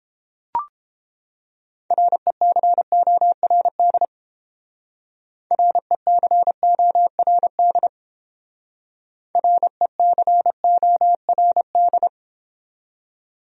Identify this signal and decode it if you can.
Morse code, heard as a keyed tone, about 30 wpm, 700 Hz. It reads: RECORD RECORD RECORD